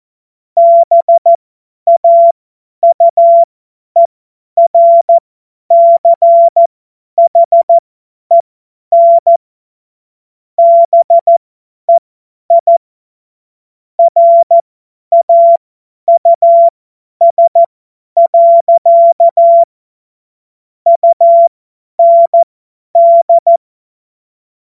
Morse code, heard as a keyed tone, about 14 wpm, spaced as for 10 wpm, 680 Hz. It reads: BAUERCHEN BEI RAUS. UND
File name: Morsekurs / Witze / Witz 27. April 2021, 50 F70